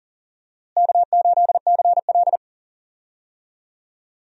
Morse code, heard as a keyed tone, about 40 wpm, 700 Hz. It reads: K8CL
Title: USA Call Signs - 4 Characters - Repeated 2x - 40wpm